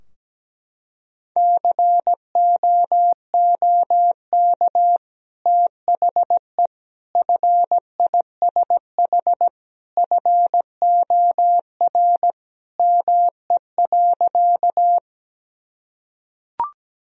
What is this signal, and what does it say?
Morse code, heard as a keyed tone, about 17 wpm, 700 Hz. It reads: COOK THE FISH FOR ME.
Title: Sentences from Top 1000 Words - Review Part 4 - 17wpm